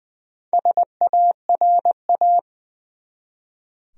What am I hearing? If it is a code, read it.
Morse code, heard as a keyed tone, about 20 wpm, 700 Hz. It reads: SARA
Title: USA Common Names - Max 4 Letters - 20wpm